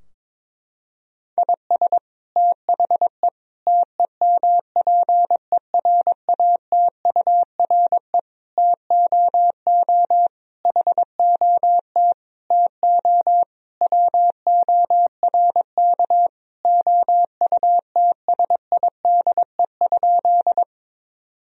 Morse code, heard as a keyed tone, about 22 wpm, 700 Hz. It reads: IS THE TEMPERATURE TOO HOT TO WORK OUTSIDE?
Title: Sentences from Top 1000 Words - Review Part 4 - 22wpm